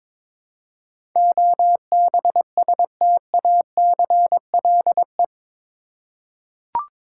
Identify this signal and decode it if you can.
Morse code, heard as a keyed tone, about 22 wpm, 700 Hz. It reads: OBSTACLE
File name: practice / 22wpm / Eight Letter Words - 22wpm